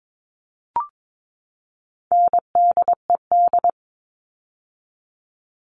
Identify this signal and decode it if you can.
Morse code, heard as a keyed tone, about 22 wpm, 700 Hz. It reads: NDED
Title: All Cut Numbers - 4-Digit 22wpm